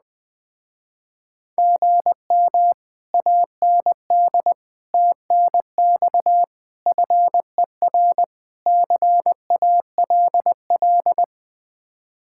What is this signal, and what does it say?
Morse code, heard as a keyed tone, about 20 wpm, 700 Hz. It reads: GM AND TNX FER CALL
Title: Speed-Racing Common QSO Phrases Encore - 17wpm